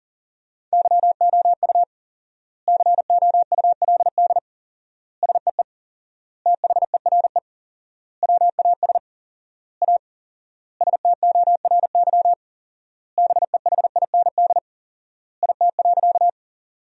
Morse code, heard as a keyed tone, about 37 wpm, 700 Hz. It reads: YOU COULD SEE THERE WAS A STORY BEHIND IT.
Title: 4x Word-Spacing - Sentences from Top 500 Words - 40wpm